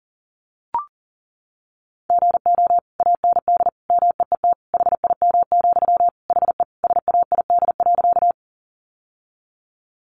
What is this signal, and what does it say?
Morse code, heard as a keyed tone, about 40 wpm, 700 Hz. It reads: GO AND MEET HIM, HE SAID.
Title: Sentences from Top 1100 Words 40wpm